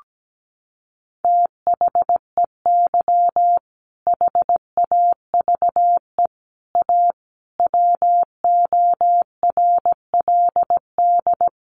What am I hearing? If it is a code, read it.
Morse code, heard as a keyed tone, about 17 wpm, 700 Hz. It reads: THEY HAVE A WORLD
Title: Sentences from Top 1500 Words 17wpm